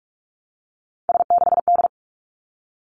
Morse code, heard as a keyed tone, about 46 wpm, 700 Hz. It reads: S6D
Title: Three Challenging Letter-Number Combinations 45wpm